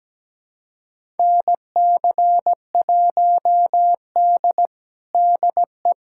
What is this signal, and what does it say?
Morse code, heard as a keyed tone, about 17 wpm, 700 Hz. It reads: NC1D DE
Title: USA Callsign DE Callsign - 4 Characters - 17wpm